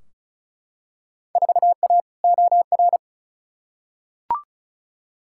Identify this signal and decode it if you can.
Morse code, heard as a keyed tone, about 36 wpm, 700 Hz. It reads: 4A OR
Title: Field Day Exchanges Encore - 35wpm